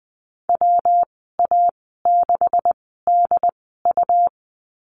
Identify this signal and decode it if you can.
Morse code, heard as a keyed tone, about 20 wpm, spaced as for 14 wpm, 700 Hz. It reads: WA6DU
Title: 186 Lesson 30g - Letter b - Call Signs 20wpm